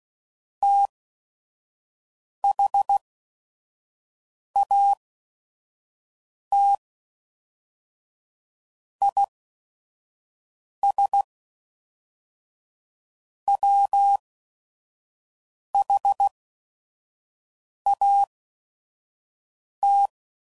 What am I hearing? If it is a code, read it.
Morse code, heard as a keyed tone, about 16 wpm, spaced as for 5 wpm, 800 Hz. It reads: THAT IS WHAT